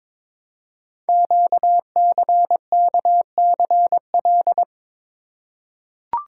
Morse code, heard as a keyed tone, about 23 wpm, 700 Hz. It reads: QCKCL
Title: Code Groups - 5 Letters - 22wpm